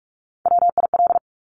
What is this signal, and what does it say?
Morse code, heard as a keyed tone, about 46 wpm, 700 Hz. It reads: WIL
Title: Top Three Letter Combinations in Words 45wpm